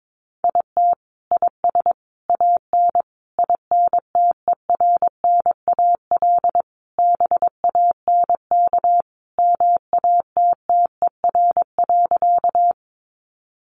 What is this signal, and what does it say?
Morse code, heard as a keyed tone, about 22 wpm, 700 Hz. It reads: IT IS AN INTERNAL BANK MATTER.
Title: Sentences from Top 1400 Words 22wpm